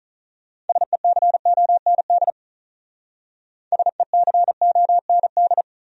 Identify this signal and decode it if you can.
Morse code, heard as a keyed tone, about 44 wpm, 700 Hz. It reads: SECOND SECOND
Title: Speed-Racing Top 300 Words 35wpm